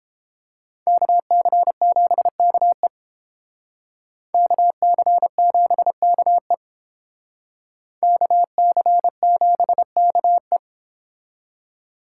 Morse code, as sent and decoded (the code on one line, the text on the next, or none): -.- -.-. --... -.- . / -.- -.-. --... -.- . / -.- -.-. --... -.- .
KC7KE KC7KE KC7KE